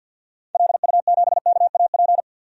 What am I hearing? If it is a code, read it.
Morse code, heard as a keyed tone, about 52 wpm, 700 Hz. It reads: LU7KAP
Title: DXCC-100 Call Signs - 6 Characters - 50wpm